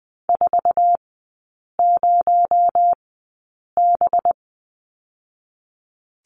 Morse code, heard as a keyed tone, about 20 wpm, spaced as for 8 wpm, 700 Hz. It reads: 40B
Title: 218 Lesson 35e - Number 0 - Three Characters 20wpm